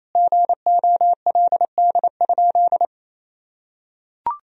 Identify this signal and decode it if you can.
Morse code, heard as a keyed tone, about 28 wpm, 700 Hz. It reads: GOLD?
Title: Sentences from Top 1000 Words - Review Part 3 - 28wpm